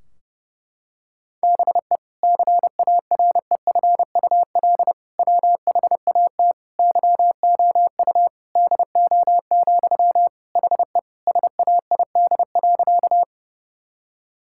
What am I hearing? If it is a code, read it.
Morse code, heard as a keyed tone, about 32 wpm, 700 Hz. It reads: BE CAREFUL WHAT YOU DO, HE SAID.
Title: Sentences from Top 1600 Words 30wpm